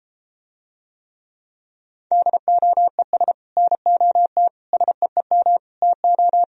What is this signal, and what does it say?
Morse code, heard as a keyed tone, about 33 wpm, 700 Hz. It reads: DOES NOT SEEM TO